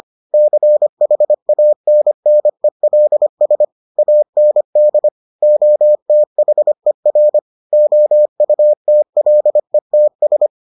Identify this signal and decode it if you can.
Morse code, heard as a keyed tone, about 25 wpm, 600 Hz. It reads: CHANNELS AND OTHER OUTLETS